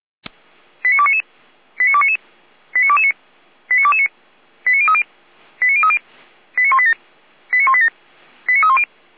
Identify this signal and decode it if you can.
Five-tone selective call (ZVEI), heard as quick runs of five tones, tones of about 70 ms. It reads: 89299 89299 89229 89229 89030 89030 89187 89187 89310